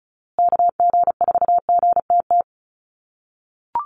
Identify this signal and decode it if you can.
Morse code, heard as a keyed tone, about 34 wpm, 700 Hz. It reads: KG4GTT
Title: USA Call Signs Repeated 2x 35wpm